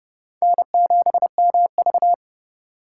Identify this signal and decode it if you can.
Morse code, heard as a keyed tone, about 37 wpm, 700 Hz. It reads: N7MV
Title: Speed Racing - USA Call Signs 30wpm